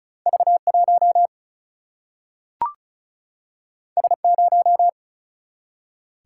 Morse code, heard as a keyed tone, about 36 wpm, 700 Hz. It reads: V1 S0
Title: Single-Letter Single-Number 35wpm